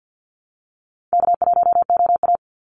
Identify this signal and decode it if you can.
Morse code, heard as a keyed tone, about 49 wpm, 700 Hz. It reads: K1OA
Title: Call Signs Repeated 3x - 50wpm